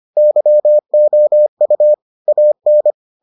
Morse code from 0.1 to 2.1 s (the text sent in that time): YOU